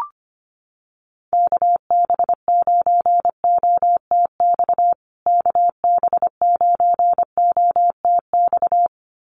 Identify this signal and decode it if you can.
Morse code, heard as a keyed tone, about 25 wpm, 700 Hz. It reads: KB9OTX KB9OTX
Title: USA Call Signs Repeated 2x 25wpm